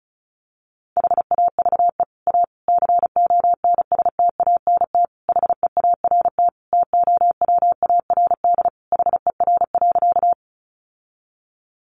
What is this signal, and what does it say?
Morse code, heard as a keyed tone, about 34 wpm, 700 Hz. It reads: HAVE A CONSTANT HEART TOWARD HER.
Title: Sentences from Top 1500 Words 35wpm